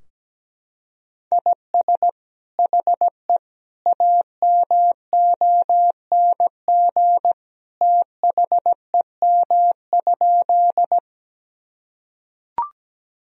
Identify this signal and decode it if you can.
Morse code, heard as a keyed tone, about 17 wpm, 700 Hz. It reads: IS HE AMONG THEM?
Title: Sentences from Top 1000 Words - Review Part 3 - 17wpm